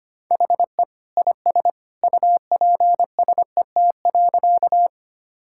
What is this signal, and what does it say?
Morse code, heard as a keyed tone, about 25 wpm, 700 Hz. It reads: HE IS UPSET.
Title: Sentences from Top 1000 Words - Review Part 5 - 25wpm